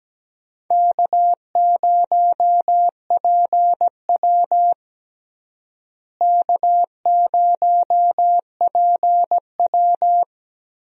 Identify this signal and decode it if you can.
Morse code, heard as a keyed tone, about 17 wpm, 700 Hz. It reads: K0PW K0PW
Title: USA Call Signs - 4 Characters - Repeated 3x - 17wpm